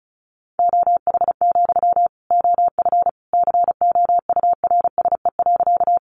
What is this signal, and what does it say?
Morse code, heard as a keyed tone, about 34 wpm, 700 Hz. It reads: OH, OF COURSE.
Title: Sentences from Top 500 Words 35wpm